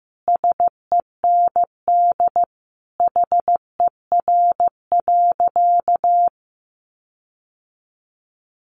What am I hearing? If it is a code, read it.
Morse code, heard as a keyed tone, about 15 wpm, 700 Hz. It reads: SEND HER.
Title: Sentences from Top 1500 Words 15wpm